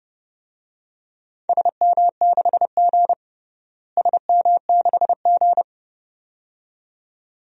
Morse code, heard as a keyed tone, about 30 wpm, 700 Hz. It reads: SM6G SM6G
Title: DXCC-100 Call Signs - 4 Characters - Repeated 2x - 30wpm